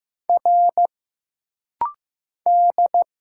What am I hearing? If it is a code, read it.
Morse code, heard as a keyed tone, about 15 wpm, 700 Hz. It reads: R D
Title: Single Letter-Number Alt - Rapid-Fire 15wpm